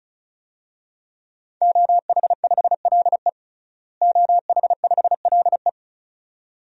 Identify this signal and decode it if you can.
Morse code, heard as a keyed tone, about 36 wpm, 700 Hz. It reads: OH5LE OH5LE
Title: DXCC-100 Call Signs - 5 Characters - Repeated 2x - 35wpm